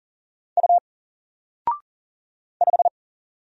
Morse code, heard as a keyed tone, about 39 wpm, 700 Hz. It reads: U 5
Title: Single Letter-Number Alt 40wpm